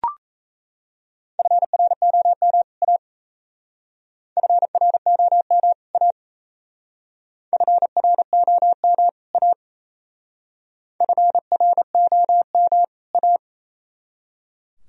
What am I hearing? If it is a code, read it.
Morse code, heard as a keyed tone, about 34 wpm, 690 Hz. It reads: FROM A FROM A FROM A FROM A
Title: Speed-Racing - Sets of 2 Words 28wpm